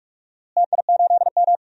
Morse code, heard as a keyed tone, about 45 wpm, 700 Hz. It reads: TI8M
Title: DXCC-100 Call Signs - 4 Characters - Repeated 3x - 45wpm